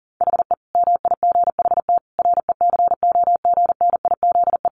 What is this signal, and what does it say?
Morse code, heard as a keyed tone, about 40 wpm, 700 Hz. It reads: HE MIGHT RECOGNIZE
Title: Sentences from Top 1400 Words 40wpm